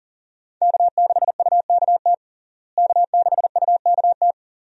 Morse code, heard as a keyed tone, about 40 wpm, 700 Hz. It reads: K6UKT K6UKT
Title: USA Call Signs - 5 Characters - Repeated 3x - 40wpm